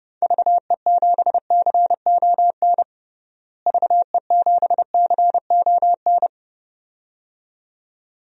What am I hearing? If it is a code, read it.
Morse code, heard as a keyed tone, about 30 wpm, 700 Hz. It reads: VE7CON VE7CON
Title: DXCC-100 Call Signs - 6 Characters - Repeated 2x - 30wpm